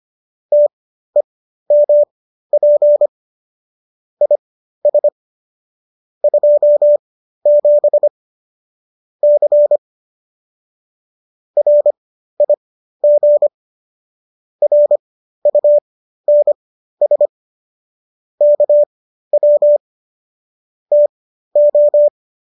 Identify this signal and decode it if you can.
Morse code, heard as a keyed tone, about 25 wpm, spaced as for 13 wpm, 590 Hz. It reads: TEMP IS 27 C RIG RUNS KW TO
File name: qso207 13